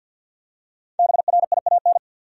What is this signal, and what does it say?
Morse code, heard as a keyed tone, about 51 wpm, 700 Hz. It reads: BRIAN
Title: USA Common Names - Max 5 Letters - 50wpm